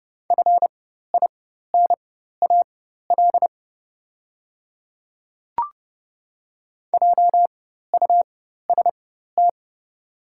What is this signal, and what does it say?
Morse code, heard as a keyed tone, about 30 wpm, spaced as for 14 wpm, 690 Hz. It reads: FINAL JUST
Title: 4x Character Spacing - Top 500 Words - 30wpm